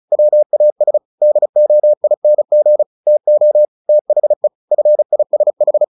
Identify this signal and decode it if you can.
Morse code, heard as a keyed tone, about 35 wpm, 600 Hz. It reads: WAS DOING TO THE FISH
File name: the old man and the sea 2 35wpm